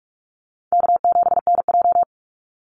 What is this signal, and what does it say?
Morse code, heard as a keyed tone, about 44 wpm, 700 Hz. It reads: K7NJ
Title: Easy Call Signs Repeated 3x - 45wpm